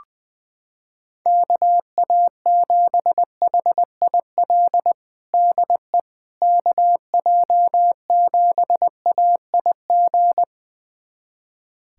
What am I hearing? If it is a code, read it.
Morse code, heard as a keyed tone, about 20 wpm, 700 Hz. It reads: KA7HIL DE KJ7AIG
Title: USA Callsign DE Callsign - 6 Characters - 20wpm